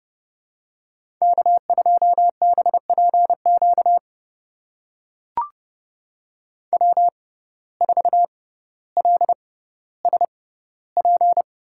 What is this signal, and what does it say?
Morse code, heard as a keyed tone, about 30 wpm, spaced as for 15 wpm, 700 Hz. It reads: K2BPQ W4LSP
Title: USA Call Signs - 5 Characters - 6x Character Spacing - 30wpm